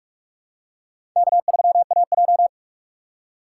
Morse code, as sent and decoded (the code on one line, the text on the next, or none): -.- ...-- .- .---
K3AJ